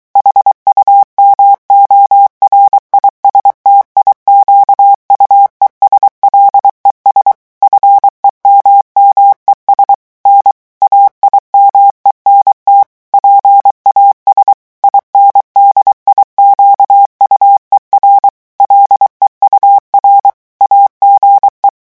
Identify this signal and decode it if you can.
Morse code, heard as a keyed tone, about 23 wpm, 800 Hz. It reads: HUMORISTIQUES LES FEMMES N AIMENT PAS INDIQUER LEUR AGE
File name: morse mp3 / revision9 citations humoristiques vitesse 22 mots minute